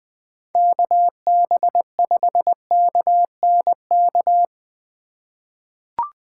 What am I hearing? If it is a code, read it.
Morse code, heard as a keyed tone, about 20 wpm, 700 Hz. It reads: KB5KNK
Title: USA Call Signs - 6 Characters - 2x Character Spacing - 20wpm